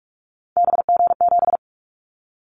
Morse code, heard as a keyed tone, about 46 wpm, 700 Hz. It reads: BG7
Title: Callsign Prefixes for Top 100 DXCC - 45wpm